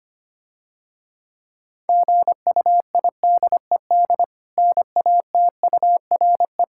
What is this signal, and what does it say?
Morse code, heard as a keyed tone, about 25 wpm, 700 Hz. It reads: GUIDED NATURE
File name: practice / 25wpm / Difficult Words - Speed-Racing with Hints - 25wpm